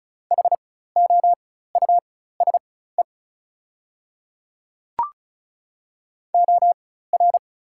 Morse code, heard as a keyed tone, about 34 wpm, spaced as for 16 wpm, 690 Hz. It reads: HOUSE OR
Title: 4x Character Spacing - Top 300 Words - 35wpm